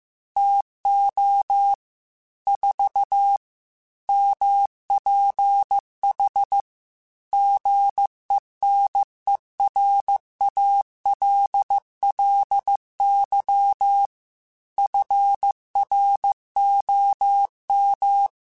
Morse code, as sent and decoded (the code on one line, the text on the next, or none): - --- / ....- / -- .--. .... / --. -. .-. .- .-.. .-.. -.-- / ..-. .-. --- --
TO 4 MPH GENERALLY FROM